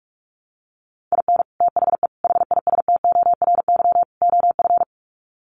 Morse code, heard as a keyed tone, about 44 wpm, 700 Hz. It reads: IN THE HISTORY OF
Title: Sets of 4 Words 45wpm